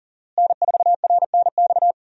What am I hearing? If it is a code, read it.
Morse code, heard as a keyed tone, about 40 wpm, 700 Hz. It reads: N4RNX
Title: USA Call Signs - 5 Characters - Repeated 2x - 40wpm